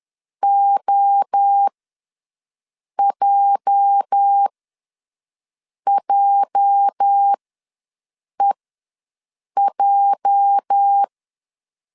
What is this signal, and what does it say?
Morse code, heard as a keyed tone, about 10 wpm, spaced as for 5 wpm, 800 Hz. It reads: OJJEJ
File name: LEZIONI / 8 YJ45 / 8 YJ ws 10wpm 4043